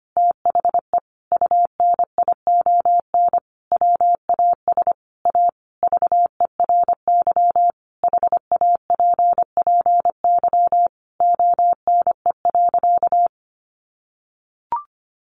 Morse code, heard as a keyed tone, about 25 wpm, 700 Hz. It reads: THE UNION WAS A VERY HAPPY ONE.
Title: Sentences from Top 1400 Words 25wpm